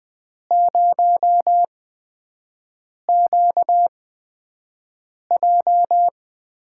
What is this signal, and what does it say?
Morse code, heard as a keyed tone, about 20 wpm, spaced as for 5 wpm, 700 Hz. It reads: 0QJ
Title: USA Call Signs - 4 Characters - 8x Character Spacing - 20wpm